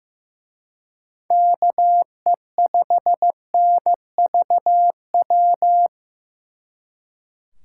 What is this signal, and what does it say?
Morse code, heard as a keyed tone, about 18 wpm, 700 Hz. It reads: KE5NVW
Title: Speed Racing - USA 2x3 Call Signs 15wpm